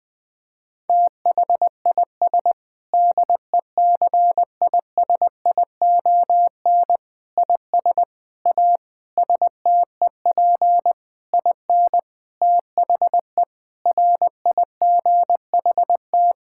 Morse code, heard as a keyed tone, about 20 wpm, 700 Hz. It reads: THIS DECISION IS A STEP IN THE RIGHT
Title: Sentences with 7-Word Phrases - 20wpm